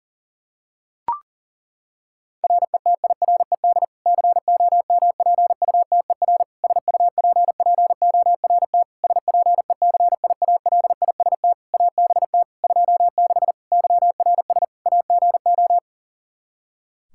Morse code, heard as a keyed tone, about 40 wpm, 700 Hz. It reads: RETIRED COMPUTER SUPPORT SPECIALIST ABT 26 YRS AGO
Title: Practice QSO - Exchange 3 - More Info - 40wpm